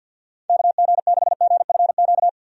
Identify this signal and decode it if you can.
Morse code, heard as a keyed tone, about 51 wpm, 700 Hz. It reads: KG6GFQ
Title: USA Call Signs - 6 Characters - 50wpm